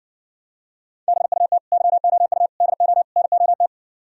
Morse code, heard as a keyed tone, about 69 wpm, 700 Hz. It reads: BUT YOU DO NOT